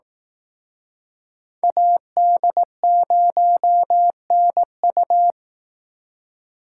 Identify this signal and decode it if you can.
Morse code, heard as a keyed tone, about 18 wpm, 700 Hz. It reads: AD0NU
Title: Speed Racing - USA 2x2 Call Signs 15wpm